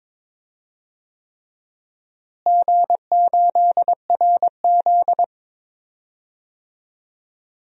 Morse code, heard as a keyed tone, about 22 wpm, 700 Hz. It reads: G8RZ
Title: DXCC-100 Call Signs - 4 Characters - Repeated 3x - 22wpm